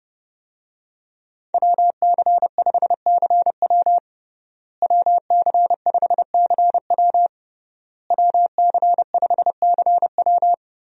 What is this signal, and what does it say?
Morse code, heard as a keyed tone, about 30 wpm, 700 Hz. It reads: WC5CW WC5CW WC5CW
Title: USA Call Signs - 5 Characters - Repeated 3x - 30wpm